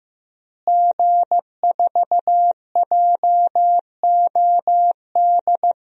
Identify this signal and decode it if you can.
Morse code, heard as a keyed tone, about 15 wpm, 700 Hz. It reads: G4JOD
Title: DXCC-100 Call Signs - 5 Characters - Repeated 3x - 15wpm